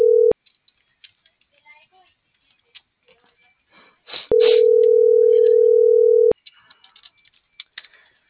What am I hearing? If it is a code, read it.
Ringback tone (North America), 440 + 480 Hz.